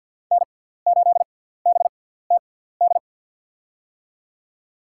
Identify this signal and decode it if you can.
Morse code, heard as a keyed tone, about 47 wpm, spaced as for 17 wpm, 700 Hz. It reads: N8BTD
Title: USA Call Signs - 5 Characters - 6x Character Spacing - 50wpm